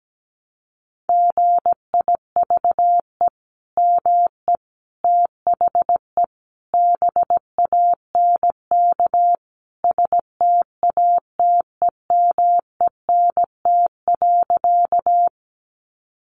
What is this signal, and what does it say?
Morse code, heard as a keyed tone, about 17 wpm, 700 Hz. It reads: GIVE ME THE BANK STATEMENT.
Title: Sentences from Top 1200 Words 17wpm